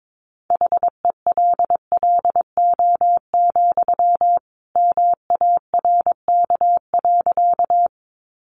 Morse code, heard as a keyed tone, about 22 wpm, 700 Hz. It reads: HELLO, MARK.
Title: Sentences from Top 1100 Words 22wpm